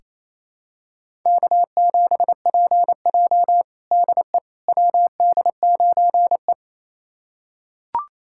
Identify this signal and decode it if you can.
Morse code, heard as a keyed tone, about 28 wpm, 700 Hz. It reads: K7PJ DE WD9E
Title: USA Callsign DE Callsign - 4 Characters - 28wpm